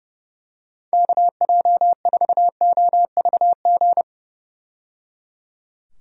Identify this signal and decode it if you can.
Morse code, heard as a keyed tone, about 30 wpm, 700 Hz. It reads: KJ4OVG
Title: USA Call Signs - 6 Characters - 30wpm